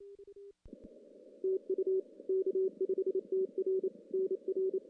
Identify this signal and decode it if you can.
Morse code, heard as a keyed tone, about 30 wpm, 380 Hz. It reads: TU K5TR NR
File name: contests / CW-SS-k5tr- -1600